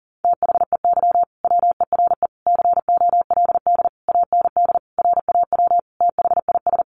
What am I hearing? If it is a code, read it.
Morse code, heard as a keyed tone, about 40 wpm, 700 Hz. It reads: THEY WERE COLD AND RAW THIS
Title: Sentences from Top 1500 Words 40wpm